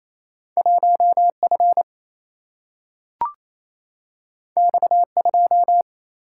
Morse code, heard as a keyed tone, about 28 wpm, 700 Hz. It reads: 1F X2
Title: Single-Letter Single-Number 28wpm